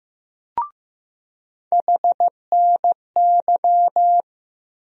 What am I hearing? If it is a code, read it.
Morse code, heard as a keyed tone, about 15 wpm, 700 Hz. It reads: HNY
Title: USA Call Sign Postfixes - 3 Letters 15wpm